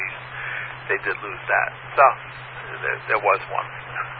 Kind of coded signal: Emergency Alert System (SAME) tones